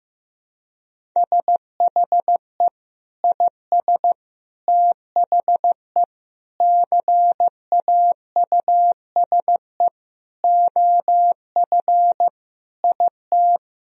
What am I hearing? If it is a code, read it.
Morse code, heard as a keyed tone, about 15 wpm, 700 Hz. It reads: SHE IS THE CAUSE OF IT
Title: Sentences from Top 1000 Words - Review Part 1 - 15wpm